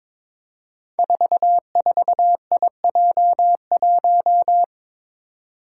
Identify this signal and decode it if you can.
Morse code, heard as a keyed tone, about 22 wpm, 700 Hz. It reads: 44IJ1
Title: Five Challenging Letter-Number Combinations 22wpm